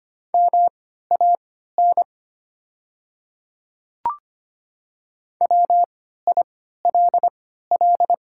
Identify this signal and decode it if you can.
Morse code, heard as a keyed tone, about 25 wpm, spaced as for 14 wpm, 690 Hz. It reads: MAN WILL
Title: 3x Character Spacing - Top 300 Words - 25wpm